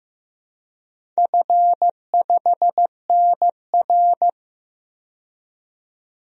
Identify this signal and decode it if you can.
Morse code, heard as a keyed tone, about 15 wpm, 700 Hz. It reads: F5NR
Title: DXCC-100 Call Signs - 4 Characters - 15wpm